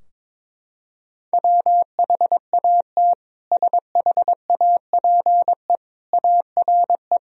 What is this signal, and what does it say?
Morse code, heard as a keyed tone, about 22 wpm, 700 Hz. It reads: WHAT SHAPE ARE
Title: Sentences from Top 1000 Words - Review Part 3 - 22wpm